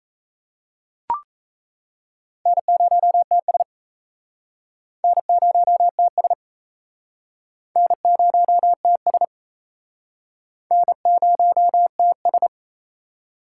Morse code, heard as a keyed tone, about 35 wpm, 700 Hz. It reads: N0TS N0TS N0TS N0TS